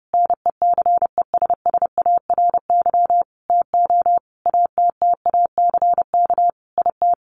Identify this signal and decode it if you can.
Morse code, heard as a keyed tone, about 30 wpm, 700 Hz. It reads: NECESSARY TO ATTACK IT
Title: Sentences from Top 1500 Words 30wpm